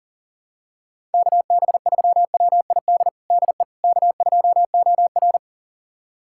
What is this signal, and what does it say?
Morse code, heard as a keyed tone, about 40 wpm, 700 Hz. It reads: KB3WID DE K2OR
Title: USA Callsign DE Callsign 40wpm